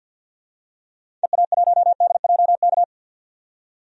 Morse code, heard as a keyed tone, about 52 wpm, 700 Hz. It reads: EA1DJK
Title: DXCC-100 Call Signs - 6 Characters - 50wpm